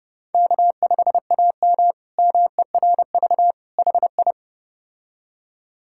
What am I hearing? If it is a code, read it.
Morse code, heard as a keyed tone, about 30 wpm, 700 Hz. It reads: K5AM MERV HI
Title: K1USN SST QSO Practice - 30wpm